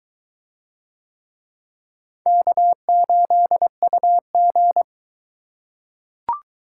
Morse code, heard as a keyed tone, about 22 wpm, 700 Hz. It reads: K8UG